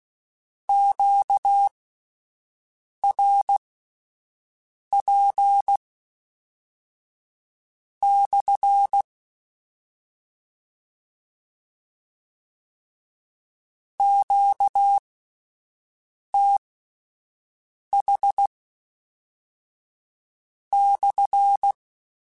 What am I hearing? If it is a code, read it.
Morse code, heard as a keyed tone, about 16 wpm, spaced as for 5 wpm, 800 Hz. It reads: QRP/ QTH/
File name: Q Calls / Q Calls-mp3s / Q Calls SPSL1 @10